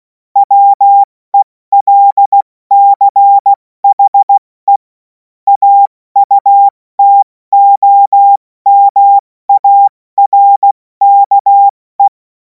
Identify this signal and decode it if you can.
Morse code, heard as a keyed tone, about 16 wpm, 800 Hz. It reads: WELCHE AUTOMARKE
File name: Morsekurs / Witze / Witz 24. März 2026 70 F80